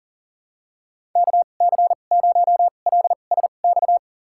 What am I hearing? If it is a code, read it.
Morse code, heard as a keyed tone, about 40 wpm, spaced as for 29 wpm, 700 Hz. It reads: KC0LSX
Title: USA Call Signs - 6 Characters - 2x Character Spacing - 40wpm